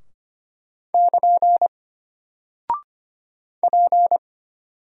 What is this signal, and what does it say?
Morse code, heard as a keyed tone, about 25 wpm, 700 Hz. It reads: <KN> P
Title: Single Character (Letter:Number:Punctuation:Prosign) Alt 25wpm